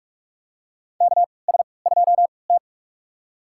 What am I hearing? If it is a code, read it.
Morse code, heard as a keyed tone, about 45 wpm, spaced as for 25 wpm, 700 Hz. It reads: KS2T
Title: USA Call Signs - 4 Characters - 3x Character Spacing - 45wpm